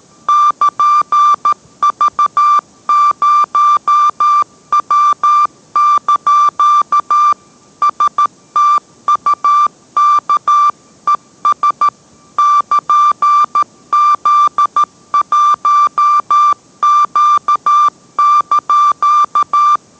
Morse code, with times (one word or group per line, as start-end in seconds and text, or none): 0.3-7.3 <KN>V0W)
7.8-11.9 STUKES
12.4-19.8 <KN>Z1Q)